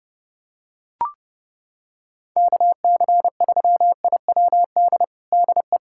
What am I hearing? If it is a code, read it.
Morse code, heard as a keyed tone, about 30 wpm, 700 Hz. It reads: KC3IWD DE
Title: USA Callsign DE Callsign - 6 Characters - 30wpm